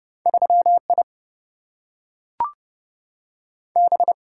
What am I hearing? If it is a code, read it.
Morse code, heard as a keyed tone, about 30 wpm, 700 Hz. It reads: 3 I B
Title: Single-Letter Single-Number 30wpm